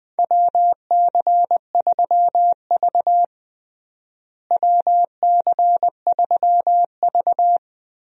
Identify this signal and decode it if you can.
Morse code, heard as a keyed tone, about 20 wpm, 690 Hz. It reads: WC3V WC3V
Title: USA Call Signs - 4 Characters - Repeated 3x - 20wpm